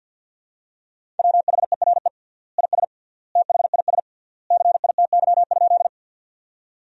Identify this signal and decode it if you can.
Morse code, heard as a keyed tone, about 52 wpm, 700 Hz. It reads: WHERE IS THIS CITY?
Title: 3x Word-Spacing - Sentences from Top 300 Words - 50wpm